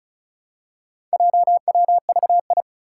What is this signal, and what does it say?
Morse code, heard as a keyed tone, about 34 wpm, 700 Hz. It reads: JWVI
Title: Four Challenging Letter-Number Combinations 35wpm